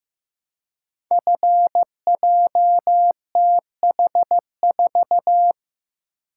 Morse code, heard as a keyed tone, about 15 wpm, 700 Hz. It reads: FJTH4
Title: Code Groups - 5 Letter:Numbers - 15wpm